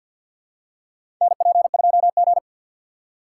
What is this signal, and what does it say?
Morse code, heard as a keyed tone, about 52 wpm, 700 Hz. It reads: NP2G
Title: USA Call Signs - 4 Characters - 6x Character Spacing - 50wpm